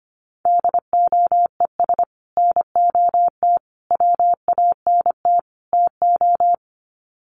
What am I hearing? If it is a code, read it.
Morse code, heard as a keyed tone, about 25 wpm, 700 Hz. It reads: DOES NOT WANT TO
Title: Sets of 4 Words 25wpm